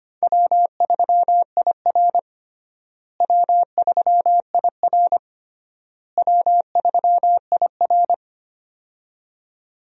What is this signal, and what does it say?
Morse code, heard as a keyed tone, about 25 wpm, 700 Hz. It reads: W3IR W3IR W3IR
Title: USA Call Signs - 4 Characters - Repeated 3x - 25wpm